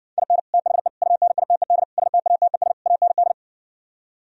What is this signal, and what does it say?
Morse code, heard as a keyed tone, about 70 wpm, 700 Hz. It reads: IN THE UNITED STATES AND